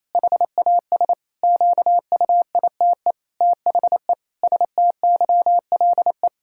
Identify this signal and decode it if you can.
Morse code, heard as a keyed tone, about 28 wpm, 700 Hz. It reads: HAS QUITE THE STYLE
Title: Sentences from Top 1000 Words - Review Part 5 - 28wpm